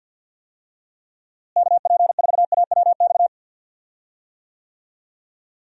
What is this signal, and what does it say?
Morse code, heard as a keyed tone, about 52 wpm, 700 Hz. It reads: KP4AWX
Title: DXCC-100 Call Signs - 6 Characters - Repeated 3x - 50wpm